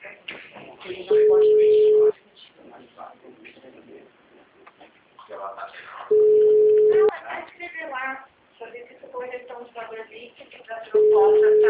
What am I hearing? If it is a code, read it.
Ringback tone (Europe, 425 Hz), with speech.
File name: out- -2021- - - .1331